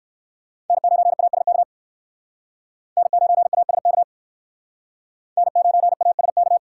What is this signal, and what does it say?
Morse code, heard as a keyed tone, about 58 wpm, 700 Hz. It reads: N9ASK N9ASK N9ASK